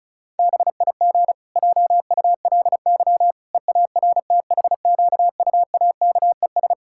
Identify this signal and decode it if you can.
Morse code, heard as a keyed tone, about 34 wpm, 700 Hz. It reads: BIG JULY EARTHQUAKES
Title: Pangram Challenge v2 - 35wpm